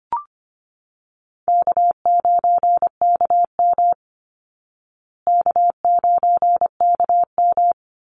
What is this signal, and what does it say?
Morse code, heard as a keyed tone, about 25 wpm, 700 Hz. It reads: K9KM K9KM
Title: Call Signs Repeated 2x - 25wpm